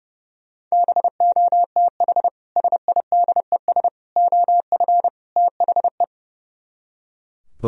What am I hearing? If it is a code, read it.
Morse code, heard as a keyed tone, about 37 wpm, 700 Hz. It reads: BOTH SIDES OF THE
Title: Speed-Racing - Sets of 4 Words 30wpm